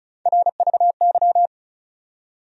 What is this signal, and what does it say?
Morse code, heard as a keyed tone, about 36 wpm, 700 Hz. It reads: RVY
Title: USA Call Sign Postfixes - 3 Letters 35wpm